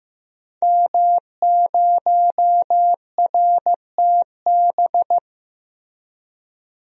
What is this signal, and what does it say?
Morse code, heard as a keyed tone, about 15 wpm, 700 Hz. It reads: M0RTB
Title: DXCC-100 Call Signs - 5 Characters - Repeated 2x - 15wpm